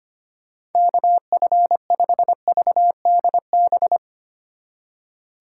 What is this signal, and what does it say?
Morse code, heard as a keyed tone, about 31 wpm, 700 Hz. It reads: KF5VDB